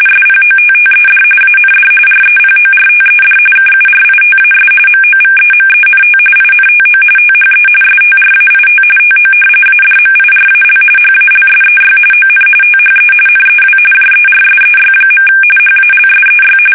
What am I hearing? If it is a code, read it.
RTTY, about 45.45 baud. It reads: SPOTTED AND REPORTED KURITAS MOVEMENT INTO THE STRAIT TO ADMIRAL HALSEY ON THE NEW JERSEY. HALSEY
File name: fsk850 / battleshipnj